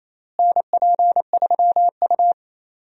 Morse code, heard as a keyed tone, about 28 wpm, 700 Hz. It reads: NP3U
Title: USA Call Signs - 4 Characters - Repeated 2x - 28wpm